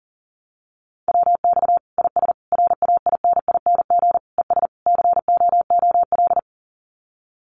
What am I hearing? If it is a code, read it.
Morse code, heard as a keyed tone, about 40 wpm, 700 Hz. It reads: WX IS RAINING ES COOL
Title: Common QSO Phrases 40wpm